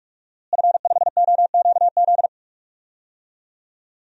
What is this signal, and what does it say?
Morse code, heard as a keyed tone, about 46 wpm, 700 Hz. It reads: F5OQZ
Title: DXCC-100 Call Signs - 5 Characters - Repeated 2x - 45wpm